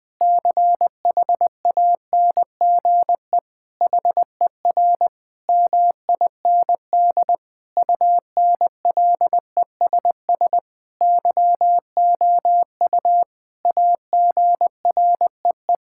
Morse code, heard as a keyed tone, about 20 wpm, 700 Hz. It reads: CHANGE HER MIND UNLESS YOU AGREE
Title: Sentences from Top 1000 Words - Review Part 6 - 20wpm